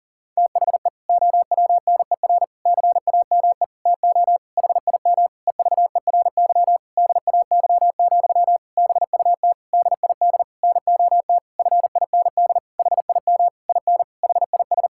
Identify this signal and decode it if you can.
Morse code, heard as a keyed tone, about 37 wpm, 700 Hz. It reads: THE OWNER CAME TO HIM EVERY DAY, BUT DID NOT FIND HIM IN HIS
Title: Sentences from Top 1600 Words 40wpm